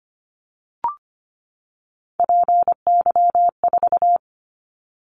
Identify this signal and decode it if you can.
Morse code, heard as a keyed tone, about 25 wpm, 700 Hz. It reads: PY4
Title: Callsign Prefixes for Top 100 DXCC - 25wpm